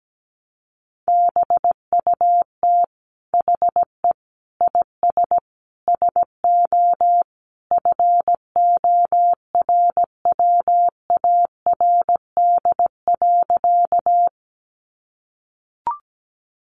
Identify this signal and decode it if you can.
Morse code, heard as a keyed tone, about 17 wpm, 700 Hz. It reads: BUT HE IS SO FORWARD.
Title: Sentences from Top 1300 Words 17wpm